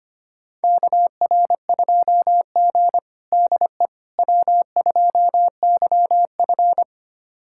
Morse code, heard as a keyed tone, about 25 wpm, 700 Hz. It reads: KR2G DE W2YF
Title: USA Callsign DE Callsign - 4 Characters - 25wpm